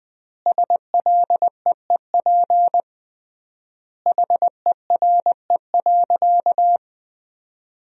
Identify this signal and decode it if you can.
Morse code, heard as a keyed tone, about 20 wpm, 700 Hz. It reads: SLEEP HERE.
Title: 3x Word-Spacing - Sentences from Top 500 Words - 20wpm